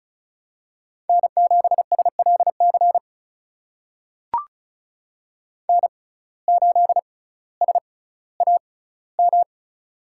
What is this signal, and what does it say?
Morse code, heard as a keyed tone, about 36 wpm, spaced as for 18 wpm, 700 Hz. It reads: N7SLC N8SAM
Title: USA Call Signs - 5 Characters - 6x Character Spacing - 35wpm